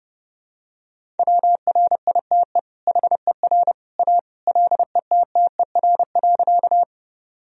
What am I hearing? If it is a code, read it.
Morse code, heard as a keyed tone, about 30 wpm, 700 Hz. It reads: WRITE HER A LETTER.
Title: Sentences from Top 1000 Words - Review Part 1 - 30wpm